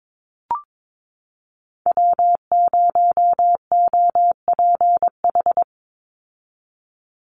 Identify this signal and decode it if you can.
Morse code, heard as a keyed tone, about 22 wpm, 700 Hz. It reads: W0OPH
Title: USA Call Signs 22wpm